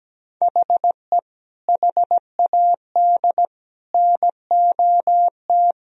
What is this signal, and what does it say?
Morse code, heard as a keyed tone, about 17 wpm, 700 Hz. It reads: HE HAD NOT
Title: Sentences from Top 1000 Words - Review Part 4 - 17wpm